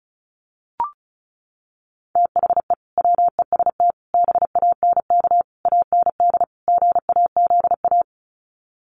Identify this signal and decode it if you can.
Morse code, heard as a keyed tone, about 34 wpm, 700 Hz. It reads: THE WEST BANK AND GAZA
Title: Sets of 5 Words 35wpm